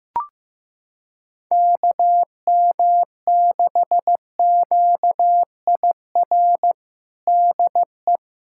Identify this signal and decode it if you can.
Morse code, heard as a keyed tone, about 15 wpm, 700 Hz. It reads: KM6QIR DE
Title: USA Callsign DE Callsign 15wpm